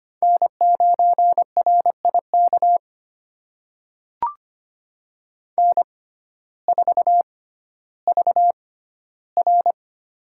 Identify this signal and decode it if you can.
Morse code, heard as a keyed tone, about 25 wpm, spaced as for 14 wpm, 700 Hz. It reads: N9RIK N4VR
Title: USA Call Signs - 6x Character Spacing - 25wpm